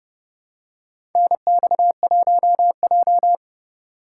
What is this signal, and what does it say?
Morse code, heard as a keyed tone, about 31 wpm, 700 Hz. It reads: NX1J
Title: USA Call Signs - 2x Character Spacing - 30wpm